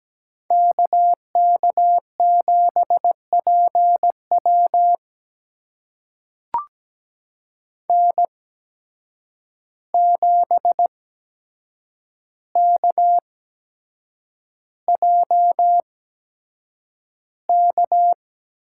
Morse code, heard as a keyed tone, about 17 wpm, spaced as for 7 wpm, 700 Hz. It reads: KK7PW N7KJK
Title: USA Call Signs - 5 Characters - 8x Character Spacing - 17wpm